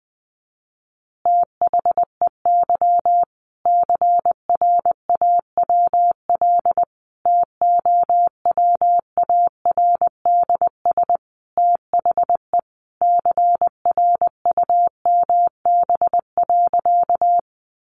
Morse code, heard as a keyed tone, about 19 wpm, 700 Hz. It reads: THEY CRAWL TOWARDS THE CRUMB.
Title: Easy Phrases 20wpm